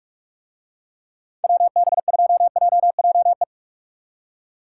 Morse code, heard as a keyed tone, about 44 wpm, 700 Hz. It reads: WB2JJE